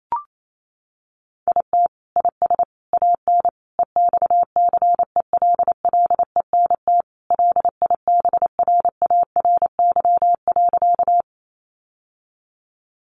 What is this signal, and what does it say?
Morse code, heard as a keyed tone, about 28 wpm, 700 Hz. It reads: IT IS AN EXCELLENT LIBRARY.
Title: Sentences from Top 1000 Words 28wpm